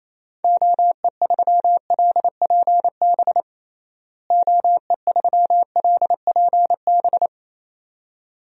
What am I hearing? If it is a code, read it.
Morse code, heard as a keyed tone, about 28 wpm, 700 Hz. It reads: OE3LPB OE3LPB
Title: DXCC-100 Call Signs - 6 Characters - Repeated 2x - 28wpm